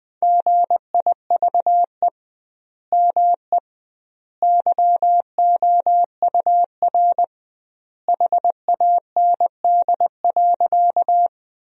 Morse code, heard as a keyed tone, about 21 wpm, 700 Hz. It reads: GIVE ME YOUR HAND.
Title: 2x Word-Spacing - Sentences from Top 200 Words - 20wpm